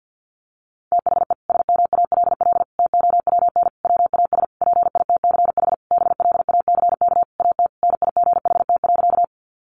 Morse code, heard as a keyed tone, about 50 wpm, 700 Hz. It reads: THE SMALL TOWN WAS PITCH BLACK AT NIGHT.
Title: Sentences from Top 1300 Words 50wpm